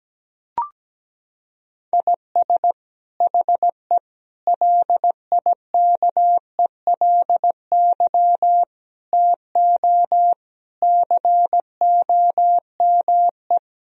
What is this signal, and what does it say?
Morse code, heard as a keyed tone, about 17 wpm, 700 Hz. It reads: IS HE LIKELY TO COME
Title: Sentences from Top 1000 Words - Review Part 4 - 17wpm